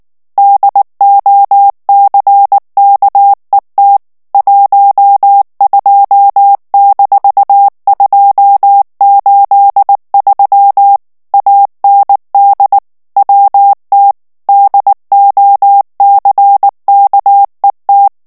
Morse code, heard as a keyed tone, about 19 wpm, 800 Hz. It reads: DOCKET 12-283 AND WT DOCKET
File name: ANGLAIS texte clair 3 vitesse 18 mots minute